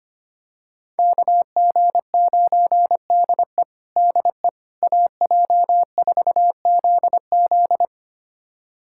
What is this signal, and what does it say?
Morse code, heard as a keyed tone, about 25 wpm, 700 Hz. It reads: KG9DE DE AJ4ZZ
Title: USA Callsign DE Callsign - 5 Characters - 25wpm